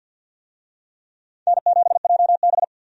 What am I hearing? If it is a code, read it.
Morse code, heard as a keyed tone, about 47 wpm, 700 Hz. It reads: N7JB
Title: USA Call Signs - 4 Characters - 50wpm